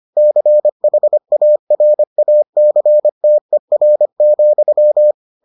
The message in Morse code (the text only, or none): CHARACTER,